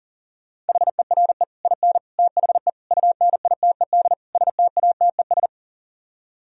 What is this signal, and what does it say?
Morse code, heard as a keyed tone, about 52 wpm, 700 Hz. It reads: HERE IN THE UNITED STATES